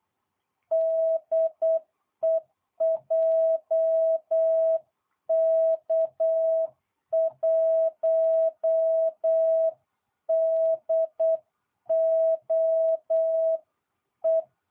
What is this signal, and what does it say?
Morse code, heard as a keyed tone, about 8 wpm, 660 Hz. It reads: DE JK1DOE